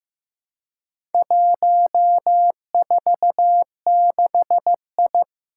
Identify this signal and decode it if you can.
Morse code, heard as a keyed tone, about 15 wpm, 700 Hz. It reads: 146I
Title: Four Challenging Letter-Number Combinations 15wpm